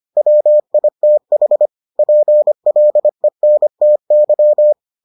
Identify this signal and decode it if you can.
Morse code, heard as a keyed tone, about 25 wpm, 600 Hz. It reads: WITH PLENTY